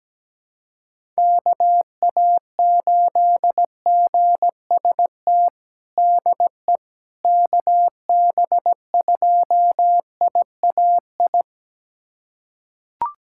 Morse code, heard as a keyed tone, about 17 wpm, 700 Hz. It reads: KA8GST DE KB2IAI
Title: USA Callsign DE Callsign - 6 Characters - 17wpm